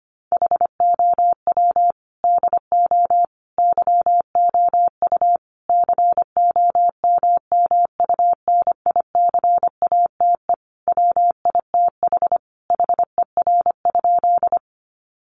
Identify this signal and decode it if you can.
Morse code, heard as a keyed tone, about 25 wpm, 700 Hz. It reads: HOW DO YOU COMMUNICATE WITH HER?
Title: Sentences from Top 1300 Words 25wpm